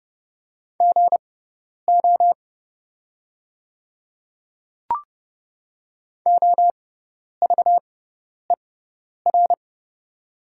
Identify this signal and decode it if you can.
Morse code, heard as a keyed tone, about 30 wpm, spaced as for 10 wpm, 690 Hz. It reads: GO OVER
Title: 6x Character Spacing - Top 100 Words - 30wpm